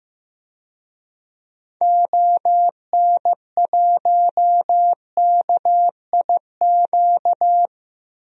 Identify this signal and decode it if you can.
Morse code, heard as a keyed tone, about 15 wpm, 700 Hz. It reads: ON1KIQ
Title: DXCC-100 Call Signs - 6 Characters - Repeated 2x - 15wpm